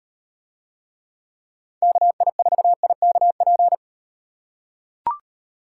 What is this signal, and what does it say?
Morse code, heard as a keyed tone, about 38 wpm, 700 Hz. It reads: KI4IKP